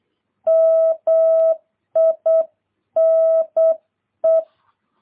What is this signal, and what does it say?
Morse code, heard as a keyed tone, about 8 wpm, 650 Hz. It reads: MINE
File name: jh8cbh / 2022cw